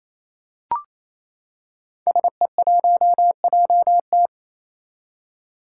Morse code, heard as a keyed tone, about 27 wpm, 700 Hz. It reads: SE1JT
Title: Five Challenging Letter-Number Combinations 28wpm